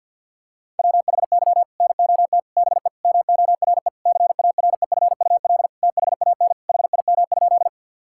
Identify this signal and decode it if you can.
Morse code, heard as a keyed tone, about 50 wpm, 700 Hz. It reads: WHY NOT BE MORE CAREFUL THAN HIM?
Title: Sentences from Top 1600 Words 50wpm